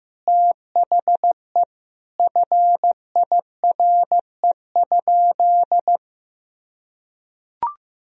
Morse code, heard as a keyed tone, about 16 wpm, 700 Hz. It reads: THE FIRE?
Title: Sentences from Top 1000 Words - Review Part 3 - 15wpm